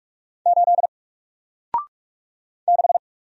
Morse code, heard as a keyed tone, about 46 wpm, 700 Hz. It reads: 8 6
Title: Single Letter-Number 45wpm